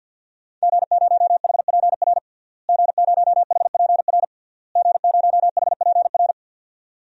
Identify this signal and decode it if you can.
Morse code, heard as a keyed tone, about 52 wpm, 700 Hz. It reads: G0HPR G0HPR G0HPR
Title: DXCC-100 Call Signs - 5 Characters - Repeated 3x - 50wpm